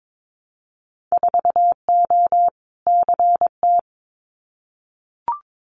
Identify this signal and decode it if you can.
Morse code, heard as a keyed tone, about 22 wpm, 700 Hz. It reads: 4O CT
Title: Winter Field Day Exchanges Encore 22wpm